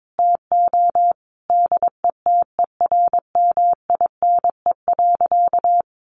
Morse code, heard as a keyed tone, about 22 wpm, 700 Hz. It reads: TO DETERMINE.
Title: Sentences from Top 1500 Words 22wpm